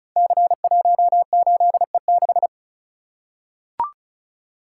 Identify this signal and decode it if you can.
Morse code, heard as a keyed tone, about 36 wpm, 700 Hz. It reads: C18E6
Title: Code Groups - 5 Letter:Numbers - 35wpm